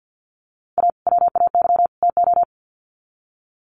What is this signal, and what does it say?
Morse code, heard as a keyed tone, about 50 wpm, 700 Hz. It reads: A WAY TO
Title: Sets of 3 Words 50wpm